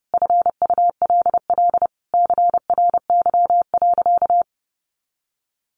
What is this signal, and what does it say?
Morse code, heard as a keyed tone, about 30 wpm, 700 Hz. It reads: FULL CRY.
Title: Sentences from Top 1500 Words 30wpm